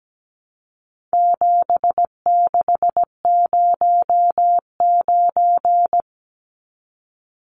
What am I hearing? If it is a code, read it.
Morse code, heard as a keyed tone, about 17 wpm, 700 Hz. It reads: 7609
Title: Numbers - 4-Digit 17wpm